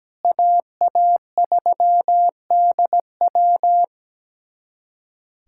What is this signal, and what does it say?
Morse code, heard as a keyed tone, about 17 wpm, 700 Hz. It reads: AA3DW
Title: USA Callsign DE Callsign - 5 Characters - 17wpm